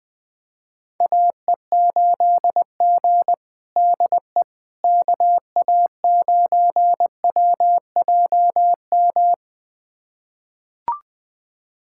Morse code, heard as a keyed tone, about 21 wpm, 700 Hz. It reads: AE8G DE KA9WJM
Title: USA Callsign DE Callsign 20wpm